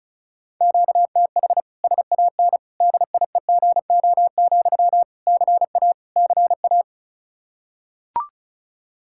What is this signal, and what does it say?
Morse code, heard as a keyed tone, about 34 wpm, 700 Hz. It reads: QTH SAN DIEGO, CA CA
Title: Practice QSO - Exchange 1 - Beginning - 35wpm